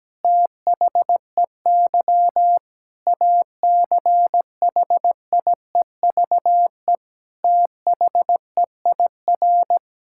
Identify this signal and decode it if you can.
Morse code, heard as a keyed tone, about 17 wpm, 700 Hz. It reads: THEY ACHIEVE THEIR
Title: Sentences from Top 1000 Words - Review Part 6 - 17wpm